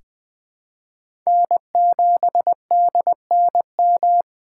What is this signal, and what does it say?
Morse code, heard as a keyed tone, about 20 wpm, 700 Hz. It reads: N7DNM
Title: USA Call Signs - 5 Characters - 4x Character Spacing - 20wpm